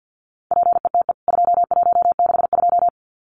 Morse code, heard as a keyed tone, about 49 wpm, 700 Hz. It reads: PETE 2162
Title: CWT Exchange 50wpm